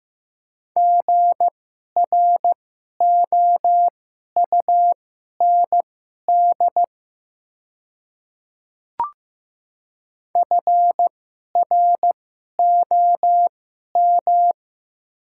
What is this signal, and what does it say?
Morse code, heard as a keyed tone, about 15 wpm, spaced as for 11 wpm, 700 Hz. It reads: GROUND FROM
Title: 2x Character Spacing - Top 500 Words - 15wpm